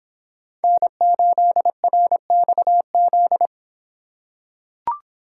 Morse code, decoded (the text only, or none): N8RXZ